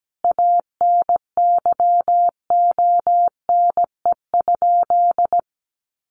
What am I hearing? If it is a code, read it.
Morse code, heard as a keyed tone, about 17 wpm, 700 Hz. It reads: ANYONE?
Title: Sentences from Top 1400 Words 17wpm